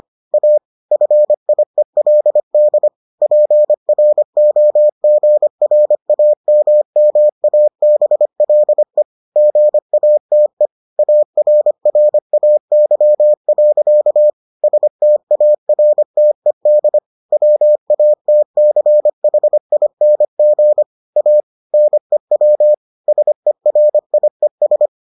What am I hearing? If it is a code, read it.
Morse code, heard as a keyed tone, about 25 wpm, 600 Hz. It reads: A FIELD PROGRAMMABLE GATE ARRAY. STARTED WATCHING A NEW SERIES